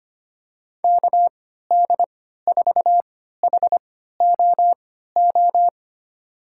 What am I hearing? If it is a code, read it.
Morse code, heard as a keyed tone, about 25 wpm, spaced as for 14 wpm, 700 Hz. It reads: KD4HOO
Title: USA Call Signs - 6 Characters - 3x Character Spacing - 25wpm